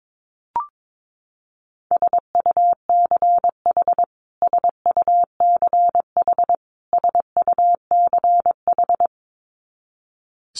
Morse code, heard as a keyed tone, about 22 wpm, 700 Hz. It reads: SUCH SUCH SUCH
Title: Top 500 Words Repeated 22wpm